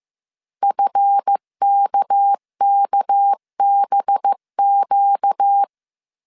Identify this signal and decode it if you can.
Morse code, heard as a keyed tone, about 15 wpm, 800 Hz. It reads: FKKBQ
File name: CW / DET / 1 9 1500C S C 14wpm 3245